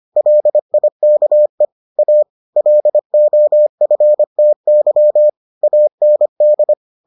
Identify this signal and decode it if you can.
Morse code, heard as a keyed tone, about 25 wpm, 600 Hz. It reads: LIKE A LOFTY AND